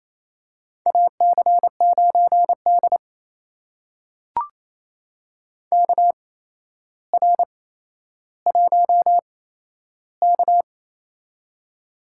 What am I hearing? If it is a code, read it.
Morse code, heard as a keyed tone, about 28 wpm, spaced as for 12 wpm, 700 Hz. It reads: AC9D KR1K
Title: USA Call Signs - 4 Characters - 8x Character Spacing - 28wpm